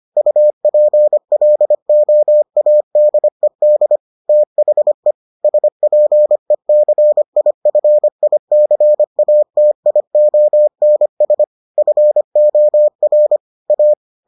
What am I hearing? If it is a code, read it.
Morse code, heard as a keyed tone, about 25 wpm, 600 Hz. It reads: UPLOADED THE SPECIFICATIONS FOR A